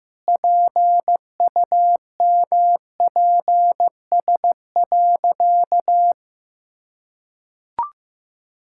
Morse code, heard as a keyed tone, about 15 wpm, 700 Hz. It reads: PUMPS.
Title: Pangram Challenge v2 - 15wpm